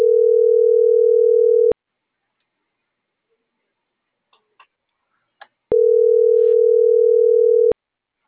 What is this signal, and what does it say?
Ringback tone (North America), 440 + 480 Hz.